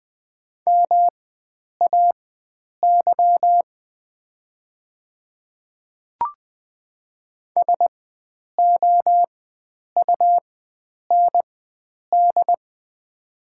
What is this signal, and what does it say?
Morse code, heard as a keyed tone, about 20 wpm, spaced as for 9 wpm, 690 Hz. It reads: MAY SOUND
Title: 4x Character Spacing - Top 100 Words - 20wpm